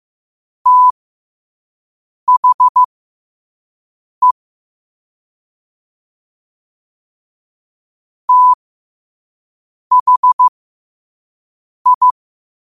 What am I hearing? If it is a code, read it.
Morse code, heard as a keyed tone, about 15 wpm, spaced as for 5 wpm, 1000 Hz. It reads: THE THI
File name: qotdfiles / 5wpm